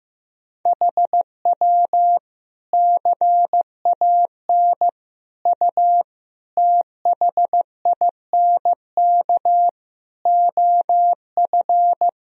Morse code, heard as a keyed tone, about 15 wpm, 700 Hz. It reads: HW CAN U THINK OF
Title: Sentences from Top 200 Words and Abbreviations 15wpm